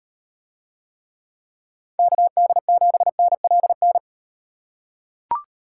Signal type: Morse code

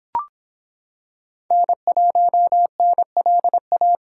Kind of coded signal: Morse code

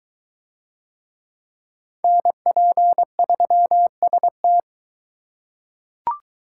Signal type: Morse code

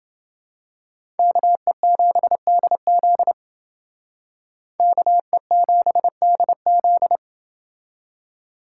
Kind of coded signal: Morse code

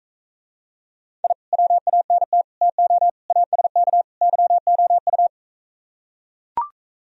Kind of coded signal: Morse code

